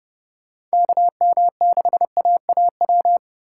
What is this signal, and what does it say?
Morse code, heard as a keyed tone, about 38 wpm, 700 Hz. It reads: KM6AAW